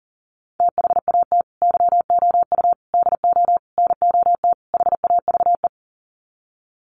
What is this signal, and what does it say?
Morse code, heard as a keyed tone, about 40 wpm, 700 Hz. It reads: THAT YOU DO NOT HAVE
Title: Sets of 5 Words Encore 40wpm